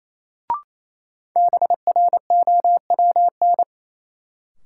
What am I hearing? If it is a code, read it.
Morse code, heard as a keyed tone, about 28 wpm, 700 Hz. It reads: BROWN
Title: Sentences from Top 1200 Words - New Word Drill - 28wpm